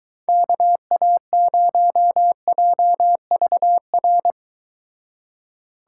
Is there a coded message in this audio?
Yes: Morse code from 0.3 to 4.3 s, heard as a keyed tone, about 24 wpm, 700 Hz. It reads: KA0JVR